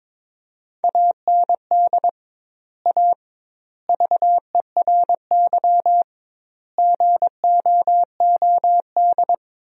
Morse code, heard as a keyed tone, about 22 wpm, 700 Hz. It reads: AND A VERY GOOD
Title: 2x Word-Spacing - Sentences from Top 200 Words - 22wpm